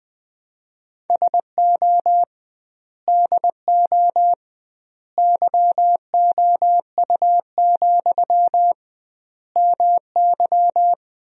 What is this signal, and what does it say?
Morse code, heard as a keyed tone, about 20 wpm, 700 Hz. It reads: SO DO YOU, MY
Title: 2x Word-Spacing - Sentences from Top 400 Words - 20wpm